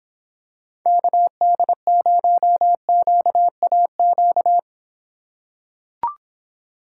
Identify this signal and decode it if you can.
Morse code, heard as a keyed tone, about 26 wpm, 700 Hz. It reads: KD0QAQ